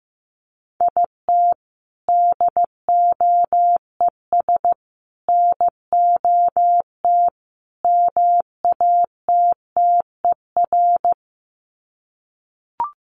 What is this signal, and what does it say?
Morse code, heard as a keyed tone, about 15 wpm, 700 Hz. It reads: IT DOES NOT MATTER
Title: Sets of 4 Words 15wpm